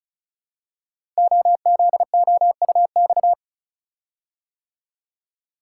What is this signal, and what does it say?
Morse code, heard as a keyed tone, about 36 wpm, 690 Hz. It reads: OZOUX
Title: Code Groups - 5 Letter:Numbers - 35wpm